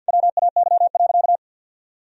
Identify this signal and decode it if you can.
Morse code, heard as a keyed tone, about 50 wpm, 700 Hz. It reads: WAY.
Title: Sentences from Top 1000 Words - Review Part 4 - 50wpm